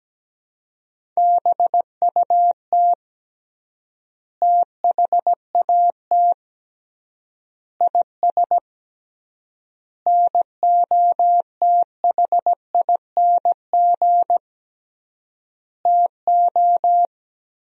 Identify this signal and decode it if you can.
Morse code, heard as a keyed tone, about 17 wpm, 700 Hz. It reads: BUT THAT IS NOTHING TO
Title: 3x Word-Spacing - Sentences from Top 500 Words - 17wpm